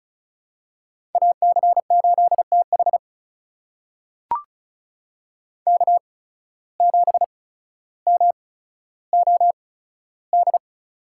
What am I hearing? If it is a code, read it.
Morse code, heard as a keyed tone, about 36 wpm, spaced as for 15 wpm, 700 Hz. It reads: AC8TH K7MOD
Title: USA Call Signs - 5 Characters - 8x Character Spacing - 35wpm